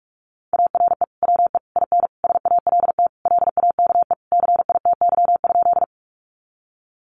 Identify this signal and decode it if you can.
Morse code, heard as a keyed tone, about 44 wpm, 700 Hz. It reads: ARE WE IN SALT LAKE CITY?
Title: Sentences from Top 1100 Words 45wpm